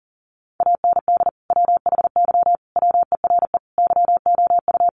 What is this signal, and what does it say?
Morse code, heard as a keyed tone, about 40 wpm, 700 Hz. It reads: AND WHY WERE YOU
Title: Sentences from Top 1000 Words 40wpm